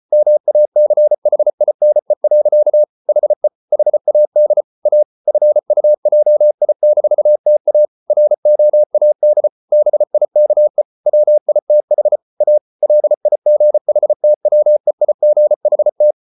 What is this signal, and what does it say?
Morse code, heard as a keyed tone, about 34 wpm, 600 Hz. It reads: MACHINE. HE HAD A FUJI-TA ROAD BIKE WITH A LIGHTWEIGHT